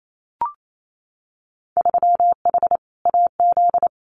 Morse code, heard as a keyed tone, about 28 wpm, 700 Hz. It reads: 3H AZ
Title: Winter Field Day Exchanges Encore 28wpm